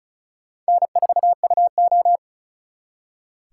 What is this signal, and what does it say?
Morse code, heard as a keyed tone, about 44 wpm, 700 Hz. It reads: N4UO